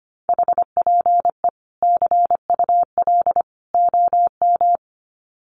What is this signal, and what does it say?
Morse code, heard as a keyed tone, about 25 wpm, 700 Hz. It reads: HPE CUL OM
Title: Common QSO Phrases 25wpm